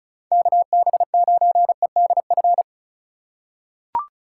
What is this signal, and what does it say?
Morse code, heard as a keyed tone, about 34 wpm, 700 Hz. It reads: KB9EDF
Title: USA Call Signs - 6 Characters - 3x Character Spacing - 35wpm